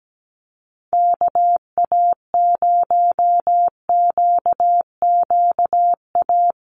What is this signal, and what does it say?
Morse code, heard as a keyed tone, about 17 wpm, 700 Hz. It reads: KA0QQA
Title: USA Call Signs Repeated 3x 17wpm